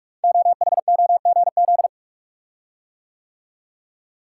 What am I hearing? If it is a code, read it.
Morse code, heard as a keyed tone, about 44 wpm, 700 Hz. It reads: OHOGZ
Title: Code Groups - 5 Letters - 45wpm